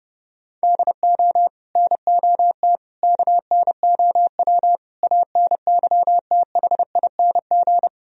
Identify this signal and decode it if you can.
Morse code, heard as a keyed tone, about 34 wpm, 700 Hz. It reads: DO NOT KNOW ANYTHING